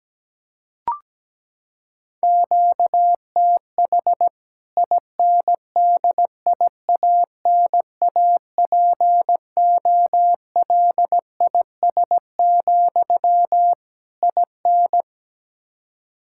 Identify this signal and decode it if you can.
Morse code, heard as a keyed tone, about 17 wpm, 700 Hz. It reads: QTH INDIANAPOLIS, IN
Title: K1USN SST QSO Practice 17wpm